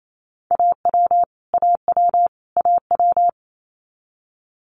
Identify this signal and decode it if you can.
Morse code, heard as a keyed tone, about 28 wpm, 700 Hz. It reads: AW AW AW
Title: Multi-letter Phonemes Repeated 3x 28wpm